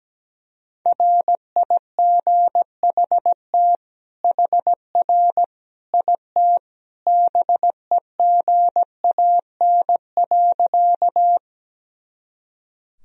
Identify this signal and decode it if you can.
Morse code, heard as a keyed tone, about 17 wpm, 700 Hz. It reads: RIGHT HR IT BEGAN.
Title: Sentences from Top 300 Words and Abbreviations 17wpm